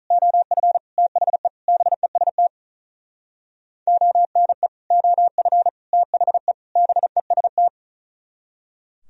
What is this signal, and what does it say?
Morse code, heard as a keyed tone, about 45 wpm, 700 Hz. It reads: OF THE BEST ONE OF THE BEST